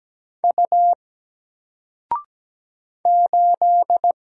Morse code, heard as a keyed tone, about 17 wpm, 700 Hz. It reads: U 8
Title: Single Character (Letter:Number:Punctuation:Prosign) 17wpm